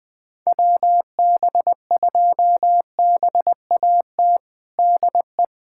Morse code, heard as a keyed tone, about 20 wpm, 700 Hz. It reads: WB2BAT DE
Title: USA Callsign DE Callsign - 6 Characters - 20wpm